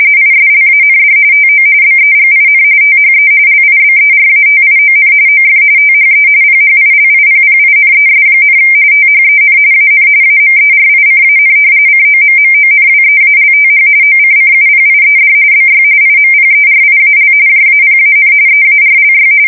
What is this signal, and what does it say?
RTTY, about 45.45 baud. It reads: ISLAND TO BEGIN THE LIBERATION OF THE PHILIPPINES, THEN AN AMERICAN COMMONWEALTH OCCUPIED BY IMPERIAL JAPAN EARLY
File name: fsk170 / battleshipnj